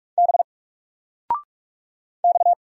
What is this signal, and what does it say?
Morse code, heard as a keyed tone, about 44 wpm, 700 Hz. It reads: B X
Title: Single Character (Letter:Number:Punctuation:Prosign) 45wpm